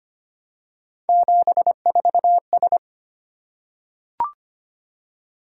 Morse code, heard as a keyed tone, about 25 wpm, 700 Hz. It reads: 74S
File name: Three Letter-Number Combinations - 25wpm